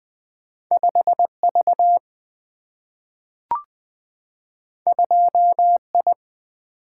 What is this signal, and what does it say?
Morse code, heard as a keyed tone, about 20 wpm, 700 Hz. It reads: 5V 2I
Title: Single-Letter Single-Number 20wpm